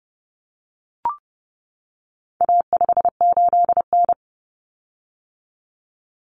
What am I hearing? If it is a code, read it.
Morse code, heard as a keyed tone, about 30 wpm, 700 Hz. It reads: A58N
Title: Common Cut Numbers - 4-Digit 30wpm